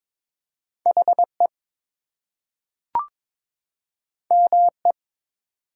Morse code, heard as a keyed tone, about 22 wpm, 700 Hz. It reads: HE ME
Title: Two Letter Words - 22wpm